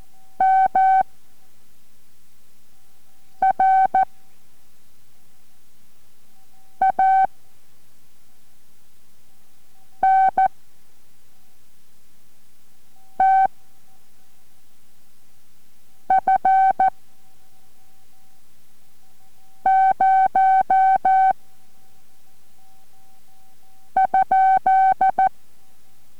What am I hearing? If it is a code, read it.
Morse code, heard as a keyed tone, about 14 wpm, 760 Hz. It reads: M R A N T F 0 ?